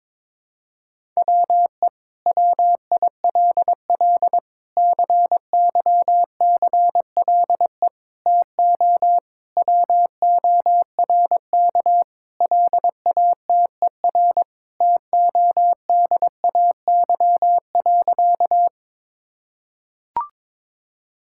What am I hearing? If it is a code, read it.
Morse code, heard as a keyed tone, about 22 wpm, 700 Hz. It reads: WE WILL CYCLE TO WORK LATER TODAY.
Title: Sentences from Top 1000 Words - Review Part 6 - 22wpm